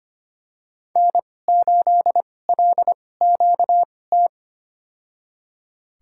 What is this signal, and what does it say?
Morse code, heard as a keyed tone, about 25 wpm, spaced as for 18 wpm, 700 Hz. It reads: N8LQT
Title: USA Call Signs - 5 Characters - 2x Character Spacing - 25wpm